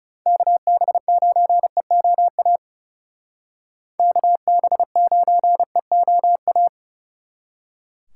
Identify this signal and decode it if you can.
Morse code, heard as a keyed tone, about 39 wpm, 700 Hz. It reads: KB9EOA KB9EOA